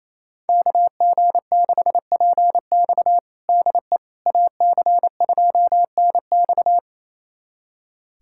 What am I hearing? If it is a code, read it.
Morse code, heard as a keyed tone, about 28 wpm, 700 Hz. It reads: KG6PX DE AC2NX
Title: USA Callsign DE Callsign - 5 Characters - 28wpm